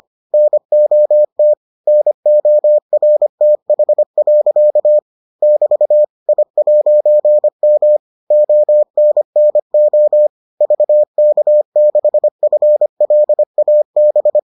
Morse code, heard as a keyed tone, about 25 wpm, 600 Hz. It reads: NOT NORTH. <BT> I'M ONNO VK6FLAB